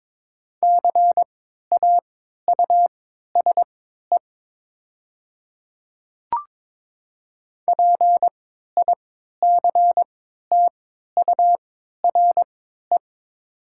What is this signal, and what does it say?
Morse code, heard as a keyed tone, about 22 wpm, spaced as for 12 wpm, 700 Hz. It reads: CAUSE PICTURE
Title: 3x Character Spacing - Top 300 Words - 22wpm